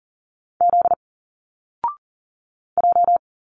Morse code, heard as a keyed tone, about 41 wpm, 700 Hz. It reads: Z J
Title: Single Letters 40wpm